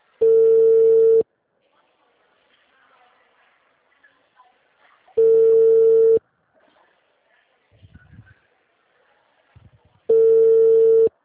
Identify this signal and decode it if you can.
Ringback tone (China, 450 Hz).